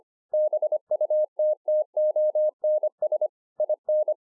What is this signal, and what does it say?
Morse code, heard as a keyed tone, about 25 wpm, 610 Hz. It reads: BUTTONS IN